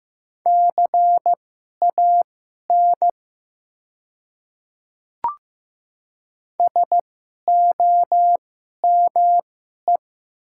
Morse code, heard as a keyed tone, about 15 wpm, spaced as for 11 wpm, 700 Hz. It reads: CAN SOME
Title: 2x Character Spacing - Top 300 Words - 15wpm